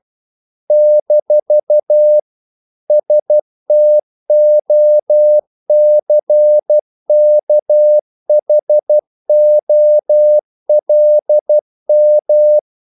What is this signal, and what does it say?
Morse code, heard as a keyed tone, about 12 wpm, 600 Hz. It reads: - STOCKHOLM
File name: CW / Practice / NationalCapitals12